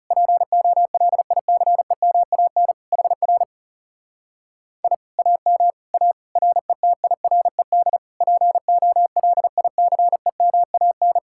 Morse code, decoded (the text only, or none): POLICEMAN HR I AM A RETIRED POLICEMAN